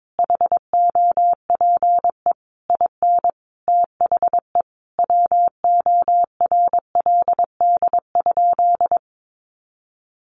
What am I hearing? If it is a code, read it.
Morse code, heard as a keyed tone, about 22 wpm, 700 Hz. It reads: HOPE IN THE WORLD?
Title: Sentences from Top 1400 Words 22wpm